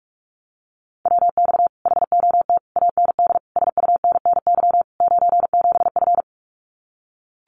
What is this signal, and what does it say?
Morse code, heard as a keyed tone, about 44 wpm, 700 Hz. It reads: WX HOT AND SUNNY 97F
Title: Common QSO Phrases Encore 45wpm